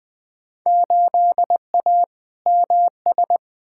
Morse code, heard as a keyed tone, about 20 wpm, 700 Hz. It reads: 8A MS
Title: Field Day Exchanges - 20wpm